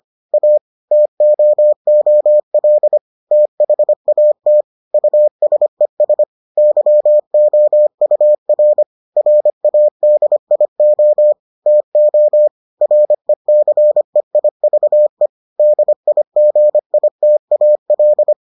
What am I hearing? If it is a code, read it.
Morse code, heard as a keyed tone, about 25 wpm, 600 Hz. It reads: A TOOL THAT USES YOUR RADIO TO RECEIVE DIGITAL